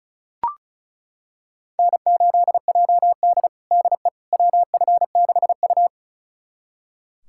Morse code, heard as a keyed tone, about 34 wpm, 700 Hz. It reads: N8JD DE WF6U
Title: USA Callsign DE Callsign - 4 Characters - 35wpm